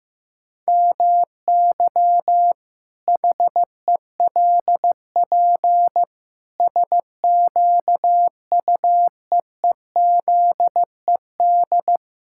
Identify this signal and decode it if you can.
Morse code, heard as a keyed tone, about 15 wpm, 700 Hz. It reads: MY HELP SQUEEZED
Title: Pangram Challenge v2 - 15wpm